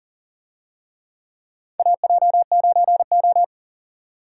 Morse code, heard as a keyed tone, about 38 wpm, 700 Hz. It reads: AJ9O
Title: USA Call Signs - 4 Characters - 40wpm